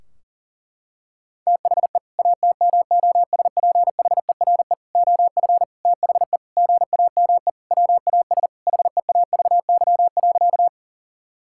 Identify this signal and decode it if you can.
Morse code, heard as a keyed tone, about 37 wpm, 700 Hz. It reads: THE ATMOSPHERE OF THE GAME WAS HEAVY.
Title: Sentences from Top 1600 Words 40wpm